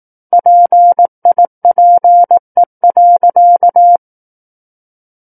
Morse code, heard as a keyed tone, about 18 wpm, 700 Hz. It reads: PIPE.
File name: 18 WPM